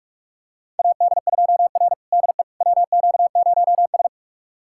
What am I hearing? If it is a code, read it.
Morse code, heard as a keyed tone, about 46 wpm, 700 Hz. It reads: AD2R DE WQ0S
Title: USA Callsign DE Callsign - 4 Characters - 45wpm